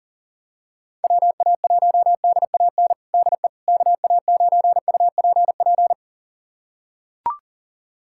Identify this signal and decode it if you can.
Morse code, heard as a keyed tone, about 42 wpm, 700 Hz. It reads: WA1DAN DE KA9UPP
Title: USA Callsign DE Callsign 40wpm